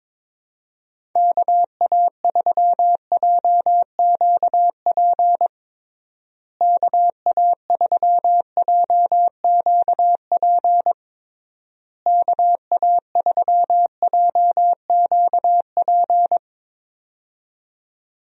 Morse code, heard as a keyed tone, about 22 wpm, 700 Hz. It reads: KA3JQP KA3JQP KA3JQP
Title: USA Call Signs - 6 Characters - Repeated 3x - 22wpm